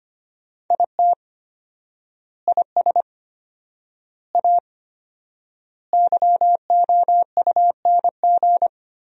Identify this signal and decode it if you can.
Morse code, heard as a keyed tone, about 25 wpm, 700 Hz. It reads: IT IS A YOUNG
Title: 4x Word-Spacing - Sentences from Top 400 Words - 25wpm